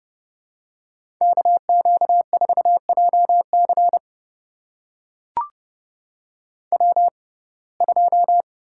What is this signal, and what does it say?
Morse code, heard as a keyed tone, about 30 wpm, spaced as for 22 wpm, 700 Hz. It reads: KQ4JC W2
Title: USA Call Signs - 5 Characters - 6x Character Spacing - 30wpm